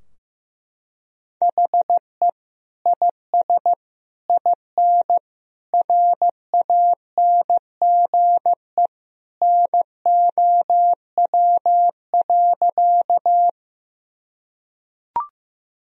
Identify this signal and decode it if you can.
Morse code, heard as a keyed tone, about 15 wpm, 700 Hz. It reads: HE IS IN RANGE NOW.
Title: Sentences from Top 1000 Words - Review Part 4 - 15wpm